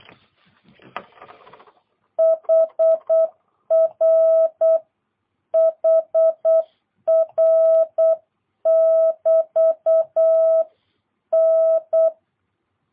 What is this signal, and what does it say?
Morse code, heard as a keyed tone, about 8 wpm, 650 Hz. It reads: HR HR <BT> N